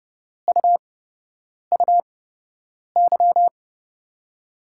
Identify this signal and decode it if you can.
Morse code, heard as a keyed tone, about 30 wpm, spaced as for 8 wpm, 700 Hz. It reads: UUY
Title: USA Call Signs - 6 Characters - 8x Character Spacing - 30wpm